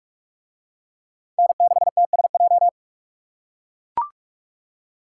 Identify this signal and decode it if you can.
Morse code, heard as a keyed tone, about 39 wpm, 700 Hz. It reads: N6TSJ